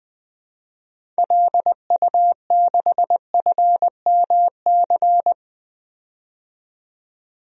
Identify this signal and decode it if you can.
Morse code, heard as a keyed tone, about 21 wpm, 700 Hz. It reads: LU6FMC
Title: DXCC-100 Call Signs - 6 Characters - 20wpm